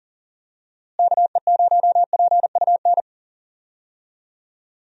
Morse code, heard as a keyed tone, about 40 wpm, 700 Hz. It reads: KE0PUN
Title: USA Call Signs - 6 Characters - Repeated 2x - 40wpm